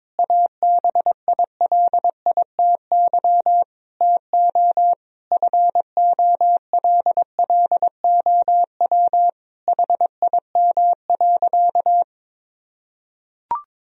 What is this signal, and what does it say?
Morse code, heard as a keyed tone, about 22 wpm, 700 Hz. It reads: ABILITY TO FOLLOW HIM.
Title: Sentences from Top 1000 Words - Review Part 3 - 22wpm